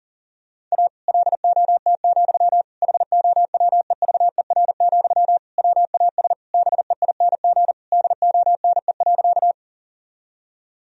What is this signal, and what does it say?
Morse code, heard as a keyed tone, about 40 wpm, 700 Hz. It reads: A LOT, HOWEVER, WAS BEING DONE.
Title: Sentences from Top 1000 Words - Review Part 3 - 40wpm